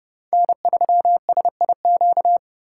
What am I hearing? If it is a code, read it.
Morse code, heard as a keyed tone, about 30 wpm, 700 Hz. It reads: N3SIQ